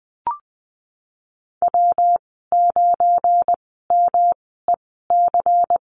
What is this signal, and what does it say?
Morse code, heard as a keyed tone, about 20 wpm, spaced as for 15 wpm, 700 Hz. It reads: W9MEC
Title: 173 Lesson 28g - Number 9 - Call Signs 20wpm